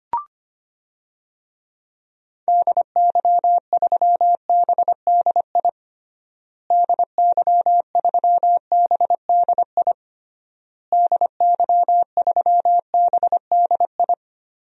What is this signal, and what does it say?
Morse code, heard as a keyed tone, about 25 wpm, 700 Hz. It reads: DY3BDI DY3BDI DY3BDI
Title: DXCC-100 Call Signs - 6 Characters - Repeated 3x - 25wpm